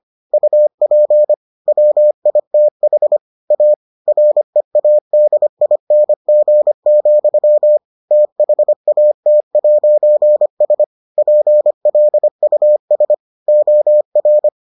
Morse code, heard as a keyed tone, about 25 wpm, 600 Hz. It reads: UP WITH A READING, THAT'S PLUS OR